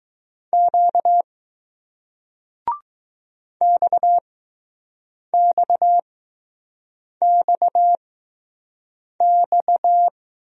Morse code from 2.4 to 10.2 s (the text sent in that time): XXXX